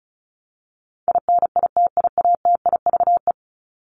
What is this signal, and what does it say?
Morse code, heard as a keyed tone, about 34 wpm, 700 Hz. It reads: INITIATIVE
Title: Top 1751-2000 Words 35wpm